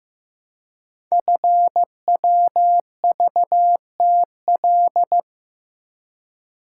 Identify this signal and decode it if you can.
Morse code, heard as a keyed tone, about 15 wpm, 700 Hz. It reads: FWVTL
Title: Code Groups - 5 Letter:Numbers - 15wpm